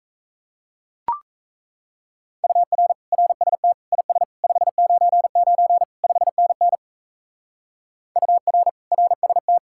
Morse code, heard as a keyed tone, about 42 wpm, 700 Hz. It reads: UR RST IS 599 5NN UR RST